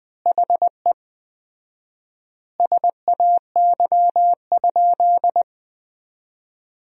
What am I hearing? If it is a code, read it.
Morse code, heard as a keyed tone, about 20 wpm, 700 Hz. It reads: HE SAY?
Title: 4x Word-Spacing - Sentences from Top 200 Words - 20wpm